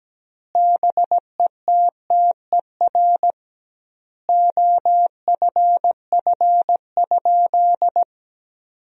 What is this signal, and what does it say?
Morse code, heard as a keyed tone, about 17 wpm, 700 Hz. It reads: BETTER OFF?
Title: 2x Word-Spacing - Sentences from Top 500 Words - 17wpm